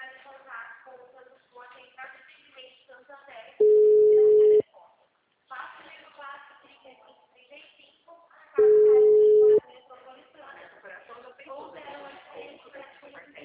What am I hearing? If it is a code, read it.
Ringback tone (Europe, 425 Hz), with speech.
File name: out- -2028- - - .47464